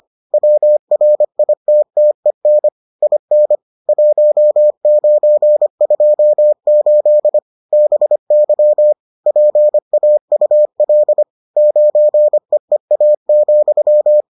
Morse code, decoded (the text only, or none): WRITTEN IN 1928 BY PAUL 9EEA,